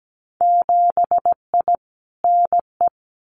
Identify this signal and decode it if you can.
Morse code, heard as a keyed tone, about 17 wpm, 700 Hz. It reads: 7I NE
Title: Winter Field Day Exchanges Encore 17wpm